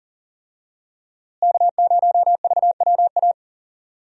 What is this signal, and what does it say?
Morse code, heard as a keyed tone, about 37 wpm, 700 Hz. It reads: K0VWA
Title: USA Call Signs - 5 Characters - Repeated 3x - 40wpm